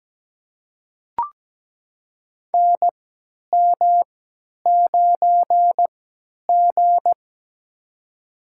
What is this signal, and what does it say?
Morse code, heard as a keyed tone, about 17 wpm, spaced as for 10 wpm, 700 Hz. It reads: NM9G
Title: USA Call Signs - 4 Characters - 3x Character Spacing - 17wpm